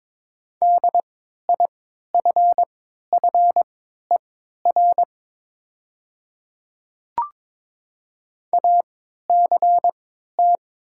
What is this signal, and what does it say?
Morse code, heard as a keyed tone, about 22 wpm, spaced as for 13 wpm, 700 Hz. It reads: DIFFER ACT
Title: 3x Character Spacing - Top 200 Words - 22wpm